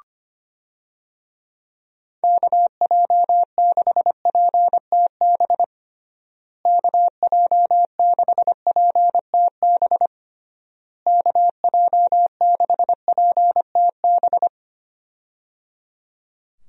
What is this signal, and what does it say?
Morse code, heard as a keyed tone, about 25 wpm, 700 Hz. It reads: KJ6PTB KJ6PTB KJ6PTB
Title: USA Call Signs - 6 Characters - Repeated 3x - 25wpm